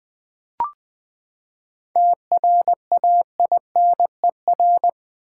Morse code, heard as a keyed tone, about 20 wpm, 700 Hz. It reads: TRAINER
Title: Seven Letter Words - 20wpm